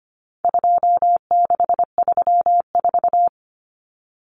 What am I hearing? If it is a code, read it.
Morse code, heard as a keyed tone, about 25 wpm, 700 Hz. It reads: 2634
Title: Numbers - 4-Digit 25wpm